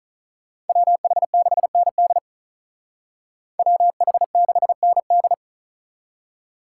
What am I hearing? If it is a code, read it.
Morse code, heard as a keyed tone, about 43 wpm, 700 Hz. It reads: WH6ND WH6ND
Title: Speed Racing - USA Call Signs 35wpm